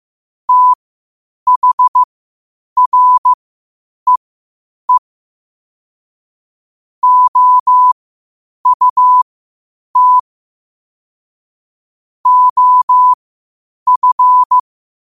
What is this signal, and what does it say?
Morse code, heard as a keyed tone, about 15 wpm, spaced as for 8 wpm, 1000 Hz. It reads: THREE OUT OF
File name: qotdfiles / 8wpm